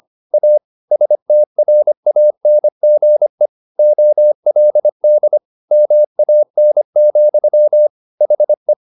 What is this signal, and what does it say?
Morse code, heard as a keyed tone, about 25 wpm, 600 Hz. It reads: A STRANGE OLD MAN, HE
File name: the old man and the sea part 1 25wpm